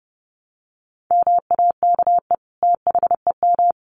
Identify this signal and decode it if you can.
Morse code, heard as a keyed tone, about 30 wpm, 700 Hz. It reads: MAKE THEM
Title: Sentences from Top 900 Words 30wpm